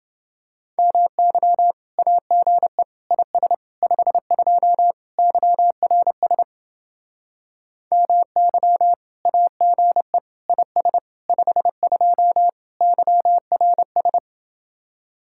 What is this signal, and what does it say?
Morse code, heard as a keyed tone, about 28 wpm, 700 Hz. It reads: MY AGE IS 52 YRS MY AGE IS 52 YRS